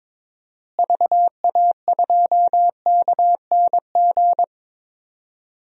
Morse code, heard as a keyed tone, about 22 wpm, 700 Hz. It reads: VA2KNG
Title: DXCC-100 Call Signs - 6 Characters - Repeated 2x - 22wpm